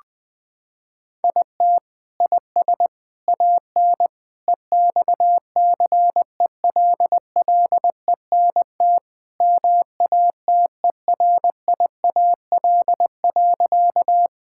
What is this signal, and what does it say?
Morse code, heard as a keyed tone, about 20 wpm, 700 Hz. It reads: IT IS AN EXCELLENT MATERIAL.
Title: Sentences from Top 1000 Words - Review Part 5 - 20wpm